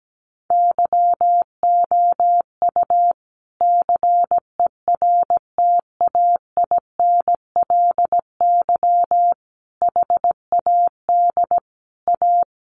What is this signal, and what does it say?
Morse code, heard as a keyed tone, about 17 wpm, 700 Hz. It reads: YOU CERTAINLY HAD A
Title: Sentences from Top 1200 Words 17wpm